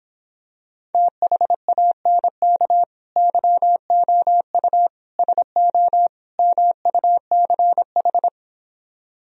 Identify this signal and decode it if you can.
Morse code, heard as a keyed tone, about 26 wpm, 700 Hz. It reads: THANK YOU SO MUCH
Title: Speed-Racing - Sets of 4 Words 22wpm